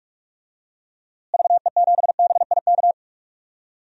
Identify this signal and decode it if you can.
Morse code, heard as a keyed tone, about 46 wpm, 700 Hz. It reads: VE7BIK
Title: DXCC-100 Call Signs - 6 Characters - Repeated 3x - 45wpm